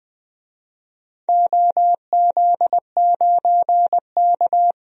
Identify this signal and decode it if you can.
Morse code, heard as a keyed tone, about 20 wpm, 700 Hz. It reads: OZ9K
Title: DXCC-100 Call Signs - 4 Characters - Repeated 3x - 20wpm